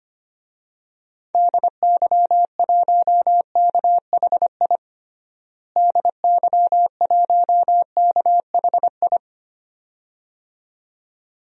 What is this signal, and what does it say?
Morse code, heard as a keyed tone, about 25 wpm, 700 Hz. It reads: DY1KHI DY1KHI
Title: DXCC-100 Call Signs - 6 Characters - Repeated 2x - 25wpm